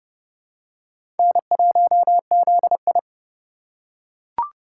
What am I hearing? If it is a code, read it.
Morse code, heard as a keyed tone, about 30 wpm, 700 Hz. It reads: N1ZI
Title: USA Call Signs - 4 Characters - 3x Character Spacing - 30wpm